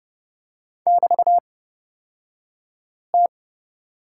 Morse code, heard as a keyed tone, about 30 wpm, spaced as for 5 wpm, 690 Hz. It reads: <BT>T
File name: Lesson 36d - BT - Two Characters 30wpm